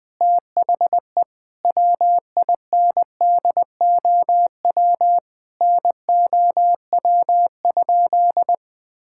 Morse code, heard as a keyed tone, about 20 wpm, 700 Hz. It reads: THE WINDOW NOW?
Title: Sentences from Top 1600 Words 20wpm